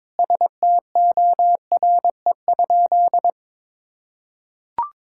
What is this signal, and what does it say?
Morse code, heard as a keyed tone, about 22 wpm, 700 Hz. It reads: STORE?
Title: Sentences from Top 1000 Words - Review Part 5 - 22wpm